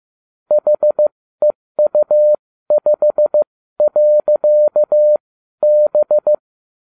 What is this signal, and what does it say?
Morse code, heard as a keyed tone, about 15 wpm, spaced as for 12 wpm, 600 Hz. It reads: HEU5.B